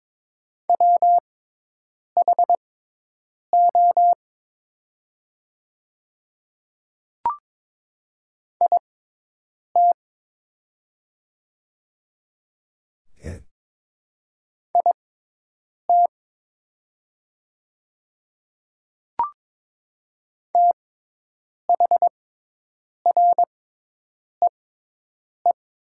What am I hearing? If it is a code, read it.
Morse code, heard as a keyed tone, about 22 wpm, spaced as for 8 wpm, 700 Hz. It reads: WHO IT IT THREE
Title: 6x Character Spacing - Top 300 Words - 22wpm